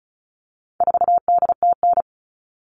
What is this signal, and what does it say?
Morse code, heard as a keyed tone, about 34 wpm, 700 Hz. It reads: 4DTN
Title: All Cut Numbers - 4-Digit 35wpm